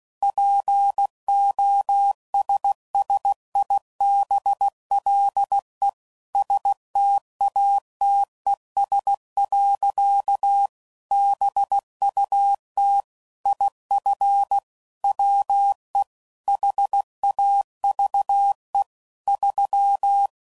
Morse code, heard as a keyed tone, about 16 wpm, 800 Hz. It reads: POSSIBLE STATES. BUT IF WE HAVE 3
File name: CW Code Groups @15WPM / Technical Subjects / AND Gate